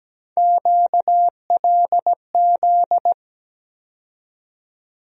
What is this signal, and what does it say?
Morse code, heard as a keyed tone, about 17 wpm, 700 Hz. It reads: QLZ
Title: USA Call Sign Postfixes - 3 Letters 17wpm